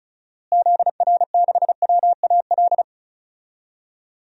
Morse code, heard as a keyed tone, about 34 wpm, 700 Hz. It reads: ZR6WAL
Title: DXCC-100 Call Signs - 6 Characters - Repeated 3x - 35wpm